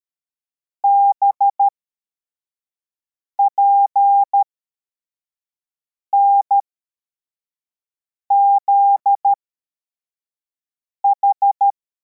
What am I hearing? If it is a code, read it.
Morse code, heard as a keyed tone, about 13 wpm, spaced as for 4 wpm, 800 Hz. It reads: BPNZH
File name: formation / cw / Text85